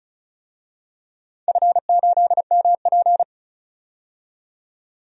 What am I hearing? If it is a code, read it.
Morse code, heard as a keyed tone, about 36 wpm, 700 Hz. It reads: F8MP
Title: DXCC-100 Call Signs - 4 Characters - 35wpm